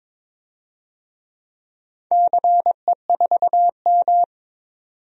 Morse code, heard as a keyed tone, about 22 wpm, 700 Hz. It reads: CE4M
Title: DXCC-100 Call Signs - 4 Characters - 22wpm